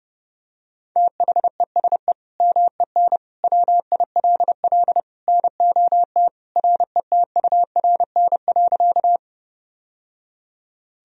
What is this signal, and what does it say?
Morse code, heard as a keyed tone, about 30 wpm, 700 Hz. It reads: THESE MEN WILL NOT RETURN.
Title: Sentences from Top 1000 Words - Review Part 1 - 30wpm